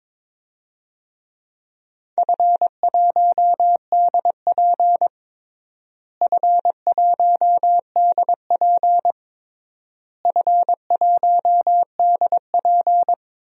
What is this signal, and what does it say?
Morse code, heard as a keyed tone, about 22 wpm, 700 Hz. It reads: F1DP F1DP F1DP
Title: DXCC-100 Call Signs - 4 Characters - Repeated 3x - 22wpm